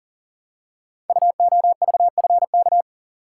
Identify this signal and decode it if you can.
Morse code, heard as a keyed tone, about 40 wpm, 690 Hz. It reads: UOVFK
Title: Code Groups - 5 Letters - 40wpm